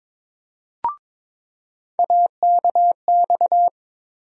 Morse code, heard as a keyed tone, about 22 wpm, 700 Hz. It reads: AKX
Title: USA Call Sign Postfixes - 3 Letters 22wpm